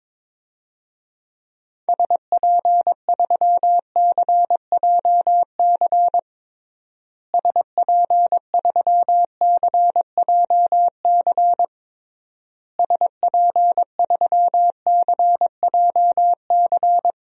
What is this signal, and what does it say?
Morse code, heard as a keyed tone, about 22 wpm, 700 Hz. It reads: SP3CJC SP3CJC SP3CJC
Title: DXCC-100 Call Signs - 6 Characters - Repeated 3x - 22wpm